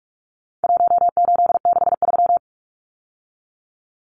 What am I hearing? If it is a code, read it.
Morse code, heard as a keyed tone, about 44 wpm, 700 Hz. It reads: 1863
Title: Numbers - Contest Style - 4-Digit 45wpm